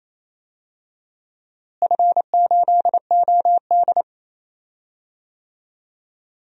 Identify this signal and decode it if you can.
Morse code, heard as a keyed tone, about 28 wpm, 700 Hz. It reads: F8OD
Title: DXCC-100 Call Signs - 4 Characters - Repeated 3x - 28wpm